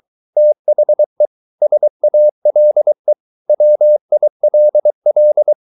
Morse code, heard as a keyed tone, about 23 wpm, 600 Hz. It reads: THE SALE WILL